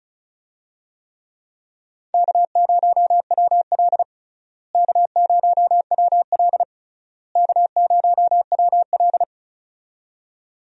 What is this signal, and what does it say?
Morse code, heard as a keyed tone, about 36 wpm, 700 Hz. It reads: K0WL K0WL K0WL
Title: USA Call Signs - 4 Characters - Repeated 3x - 35wpm